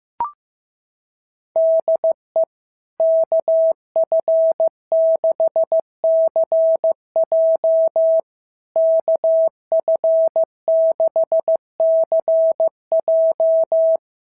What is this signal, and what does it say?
Morse code, heard as a keyed tone, about 15 wpm, 650 Hz. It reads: DE KF6CJ KF6CJ
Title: SOTA QSO Practice 15wpm